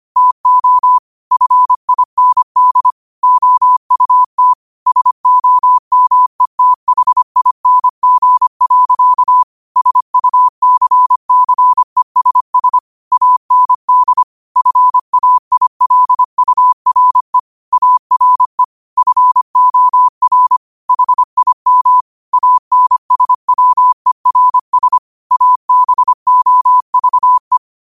Morse code, heard as a keyed tone, about 25 wpm, 1000 Hz. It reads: TO FIND OUT SOMETHING. SUCCESS AND FAILURE ARE FOR HIM ANSWERS ABOVE
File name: qotdfiles / 25wpm